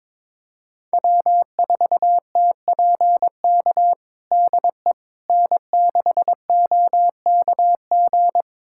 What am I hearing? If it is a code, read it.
Morse code, heard as a keyed tone, about 22 wpm, 700 Hz. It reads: W4TPK DE N6OKG
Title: USA Callsign DE Callsign - 5 Characters - 22wpm